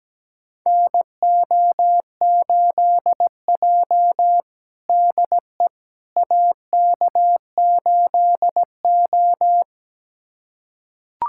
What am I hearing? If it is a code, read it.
Morse code, heard as a keyed tone, about 17 wpm, 700 Hz. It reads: NO8J DE AK8O
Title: USA Callsign DE Callsign - 4 Characters - 17wpm